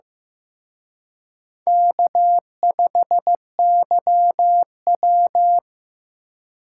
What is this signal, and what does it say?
Morse code, heard as a keyed tone, about 18 wpm, 700 Hz. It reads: K5YW
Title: Speed Racing - USA 1x2 Call Signs 15wpm